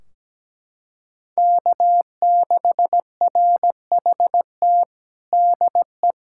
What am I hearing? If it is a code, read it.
Morse code, heard as a keyed tone, about 17 wpm, 700 Hz. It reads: K6RHT DE
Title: K1USN SST QSO Practice 17wpm